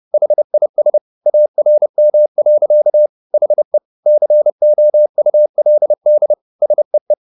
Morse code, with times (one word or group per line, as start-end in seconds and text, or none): 0.1-1.0 HIS
1.3-3.1 ARM.
3.3-3.8 HE
4.1-6.3 COULD
6.6-7.1 SEE